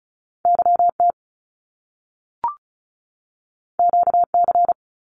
Morse code, heard as a keyed tone, about 36 wpm, 700 Hz. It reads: YT QC
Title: Canadian Provinces and Territories 35wpm